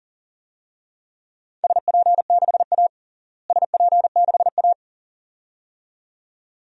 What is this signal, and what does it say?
Morse code, heard as a keyed tone, about 40 wpm, 700 Hz. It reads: SP6A SP6A
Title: DXCC-100 Call Signs - 4 Characters - Repeated 2x - 40wpm